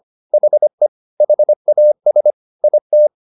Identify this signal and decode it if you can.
Morse code, heard as a keyed tone, about 25 wpm, 600 Hz. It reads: HE HAS IT
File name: the old man and the sea part 1 25wpm